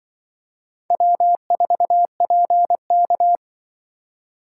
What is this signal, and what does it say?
Morse code, heard as a keyed tone, about 23 wpm, 700 Hz. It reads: W4PK
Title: Speed Racing - USA 1x2 Call Signs 20wpm